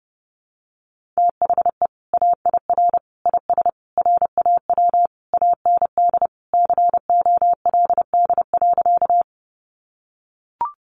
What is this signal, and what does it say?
Morse code, heard as a keyed tone, about 30 wpm, 700 Hz. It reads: THE AIR IS RAW AND COLD.
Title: Sentences from Top 1500 Words 30wpm